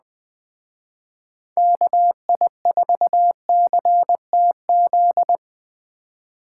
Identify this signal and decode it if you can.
Morse code, heard as a keyed tone, about 20 wpm, 700 Hz. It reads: KI4CTZ